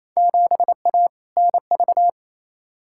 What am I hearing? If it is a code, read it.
Morse code, heard as a keyed tone, about 28 wpm, 700 Hz. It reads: 7A NV
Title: Field Day Exchanges Encore - 28wpm